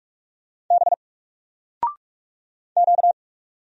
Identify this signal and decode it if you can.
Morse code, heard as a keyed tone, about 44 wpm, 700 Hz. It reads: B Q
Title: Single Character (Letter:Number:Punctuation:Prosign) 45wpm